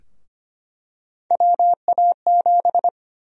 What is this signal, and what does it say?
Morse code, heard as a keyed tone, about 25 wpm, 700 Hz. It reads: WA7
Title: USA Call Sign Prefixes - Repeated 2x - 25wpm